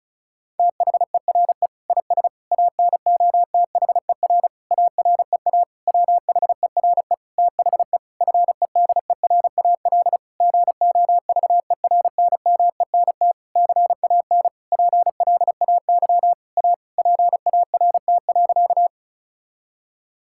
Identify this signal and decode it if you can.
Morse code, heard as a keyed tone, about 36 wpm, 700 Hz. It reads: THERE IS ANOTHER AREA WHERE THE FEDERAL GOVERNMENT CAN PLAY A PART.
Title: Sentences from Top 1000 Words - Review Part 5 - 35wpm